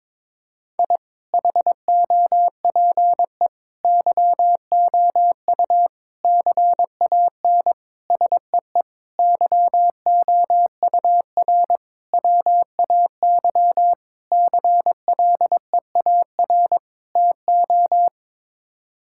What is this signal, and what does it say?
Morse code, heard as a keyed tone, about 22 wpm, 700 Hz. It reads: I HOPE YOU CAN SEE YOUR WAY CLEAR TO
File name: Sets of 9 Words - 22wpm